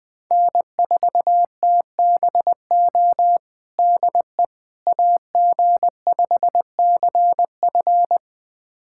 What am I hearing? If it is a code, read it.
Morse code, heard as a keyed tone, about 20 wpm, 700 Hz. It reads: N4TBO DE AG5CF
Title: USA Callsign DE Callsign - 5 Characters - 20wpm